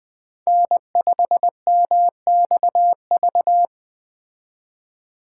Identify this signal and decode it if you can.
Morse code, heard as a keyed tone, about 20 wpm, 700 Hz. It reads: N5MXV
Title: Speed Racing - USA 1x3 Call Signs 17wpm